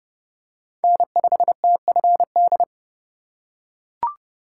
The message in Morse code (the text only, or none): N5TFD